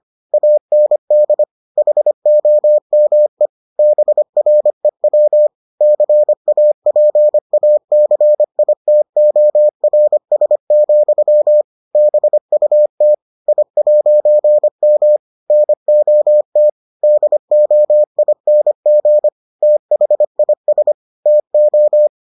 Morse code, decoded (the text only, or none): AND HOME BREW CAPACITORS, BUT I'M NOT DOING THIS TO